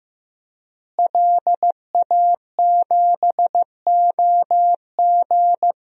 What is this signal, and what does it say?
Morse code, heard as a keyed tone, about 15 wpm, 700 Hz. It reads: LA7OG
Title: DXCC-100 Call Signs - 5 Characters - Repeated 3x - 15wpm